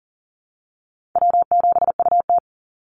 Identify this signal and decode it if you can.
Morse code, heard as a keyed tone, about 40 wpm, 700 Hz. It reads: W7UT
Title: Easy Call Signs Repeated 3x - 40wpm